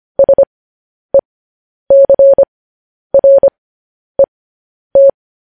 Morse code, heard as a keyed tone, about 25 wpm, spaced as for 10 wpm, 550 Hz. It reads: SECRET
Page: news-headlines-in-morse-code-at-10-wpm